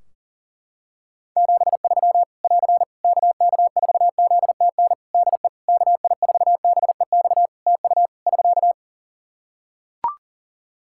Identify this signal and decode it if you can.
Morse code, heard as a keyed tone, about 42 wpm, 700 Hz. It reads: 73 <AR> KK4ZTN DE KI4BEX TU <SK>
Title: K1USN SST QSO Practice 40wpm